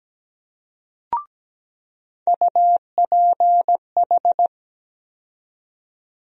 Morse code, heard as a keyed tone, about 17 wpm, 700 Hz. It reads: UPH
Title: USA Call Sign Postfixes - 3 Letters 17wpm